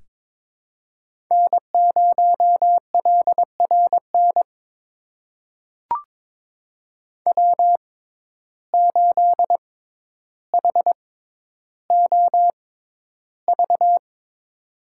Morse code, heard as a keyed tone, about 21 wpm, spaced as for 11 wpm, 700 Hz. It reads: N0LRN W8HOV
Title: USA Call Signs - 5 Characters - 6x Character Spacing - 22wpm